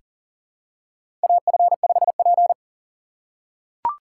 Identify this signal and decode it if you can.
Morse code, heard as a keyed tone, about 40 wpm, 700 Hz. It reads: AF5P
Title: K1USN SST QSO Practice - 40wpm